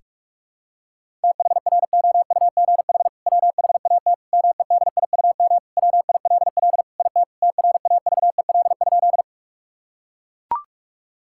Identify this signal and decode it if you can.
Morse code, heard as a keyed tone, about 41 wpm, 700 Hz. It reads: THROUGH WHAT MEDIUM WILL IT TRAVEL?
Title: Sentences from Top 1000 Words - Review Part 6 - 45wpm